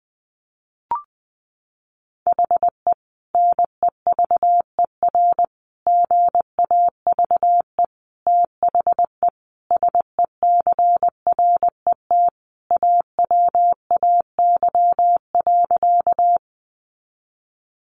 Morse code, heard as a keyed tone, about 20 wpm, 700 Hz. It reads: HE NEVER GAVE THE SECRET AWAY.
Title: Sentences from Top 1200 Words 20wpm